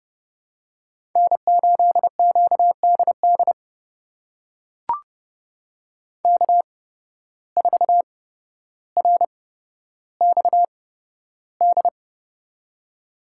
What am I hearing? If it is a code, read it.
Morse code, heard as a keyed tone, about 31 wpm, spaced as for 13 wpm, 700 Hz. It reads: N8QDD K4RXD
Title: USA Call Signs - 5 Characters - 8x Character Spacing - 30wpm